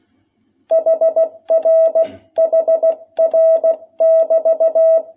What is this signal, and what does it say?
Morse code, heard as a keyed tone, about 16 wpm, 660 Hz. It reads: HR HR <BT>
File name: jh8cbh / 2021cw